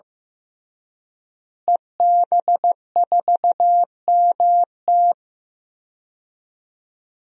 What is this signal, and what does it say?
Morse code, heard as a keyed tone, about 15 wpm, 700 Hz. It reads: EB4MT
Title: DXCC-100 Call Signs - 5 Characters - Repeated 3x - 15wpm